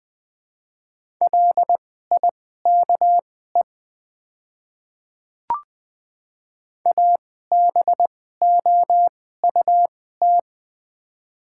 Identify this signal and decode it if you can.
Morse code, heard as a keyed tone, about 20 wpm, spaced as for 14 wpm, 700 Hz. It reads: LIKE ABOUT
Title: 2x Character Spacing - Top 100 Words - 20wpm